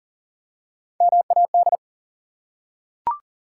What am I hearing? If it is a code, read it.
Morse code, heard as a keyed tone, about 42 wpm, 700 Hz. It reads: MAD
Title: Three Letter Words - 40wpm